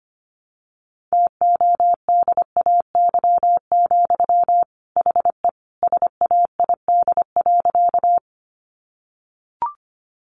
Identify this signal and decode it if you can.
Morse code, heard as a keyed tone, about 25 wpm, 700 Hz. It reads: TODAY, HE SAID.
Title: Sentences from Top 600 Words 25wpm